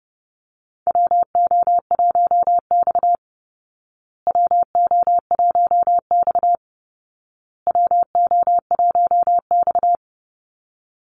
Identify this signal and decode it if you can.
Morse code, heard as a keyed tone, about 30 wpm, 700 Hz. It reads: WO1X WO1X WO1X
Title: Call Signs Repeated 2x - 30wpm